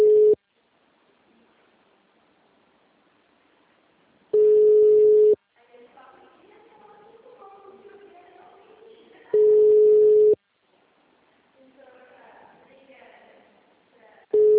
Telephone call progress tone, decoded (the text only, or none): ringback tone